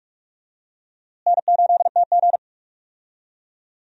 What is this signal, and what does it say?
Morse code, heard as a keyed tone, about 46 wpm, 700 Hz. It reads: N8TG
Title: USA Call Signs - 4 Characters - Repeated 3x - 45wpm